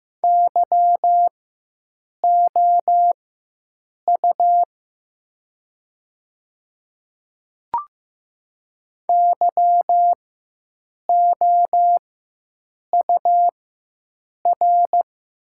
Morse code, heard as a keyed tone, about 15 wpm, spaced as for 7 wpm, 690 Hz. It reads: YOU YOUR
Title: 4x Character Spacing - Top 500 Words - 15wpm